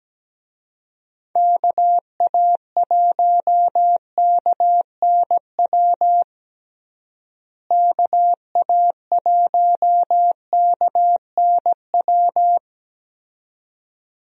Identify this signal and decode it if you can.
Morse code, heard as a keyed tone, about 17 wpm, 700 Hz. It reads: KA1KNW KA1KNW
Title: USA Call Signs - 6 Characters - Repeated 2x - 17wpm